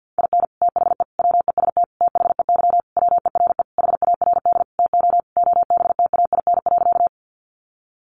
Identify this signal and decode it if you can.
Morse code, heard as a keyed tone, about 48 wpm, 700 Hz. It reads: IN THE WEST THEY WERE HARD TO OBTAIN.
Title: Sentences from Top 1300 Words 50wpm